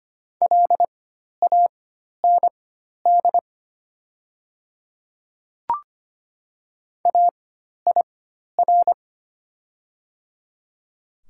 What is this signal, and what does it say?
Morse code, heard as a keyed tone, about 25 wpm, spaced as for 12 wpm, 690 Hz. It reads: LAND AIR
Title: 4x Character Spacing - Top 500 Words - 25wpm